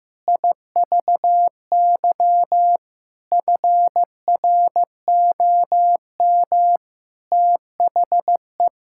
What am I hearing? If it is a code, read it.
Morse code, heard as a keyed tone, about 16 wpm, 700 Hz. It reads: IVY FROM THE
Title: Pangram Challenge v2 - 15wpm